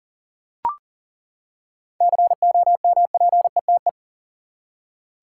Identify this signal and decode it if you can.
Morse code, heard as a keyed tone, about 40 wpm, 700 Hz. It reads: COMPETE
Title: Seven Letter Words - 40wpm